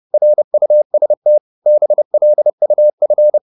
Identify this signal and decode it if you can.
Morse code, heard as a keyed tone, about 30 wpm, 600 Hz. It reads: RUST BLUF
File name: Korte ord DK 30wpm